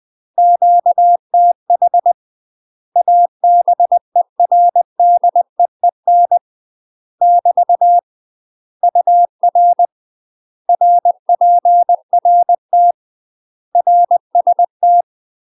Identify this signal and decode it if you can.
Morse code, heard as a keyed tone, about 20 wpm, 700 Hz. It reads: QTH ABERDEEN <BT> UR RPRT RST
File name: UK MASTER QSOs TRAINING / 20WPM / UK MASTER QSOs 0001